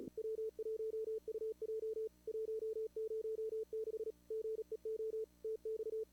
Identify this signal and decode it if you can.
Morse code, heard as a keyed tone, about 40 wpm, 440 Hz. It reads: W1UJ 106 GEO TX
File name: CWSprint / Feb2022 / k5tr-feb-na-sprint-cw-k5ot -0000